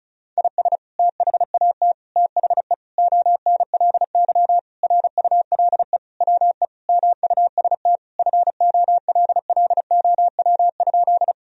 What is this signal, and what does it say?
Morse code, heard as a keyed tone, about 36 wpm, 700 Hz. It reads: IS THAT THE ONLY RULE WE MUST FOLLOW?
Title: Sentences from Top 1000 Words - Review Part 2 - 35wpm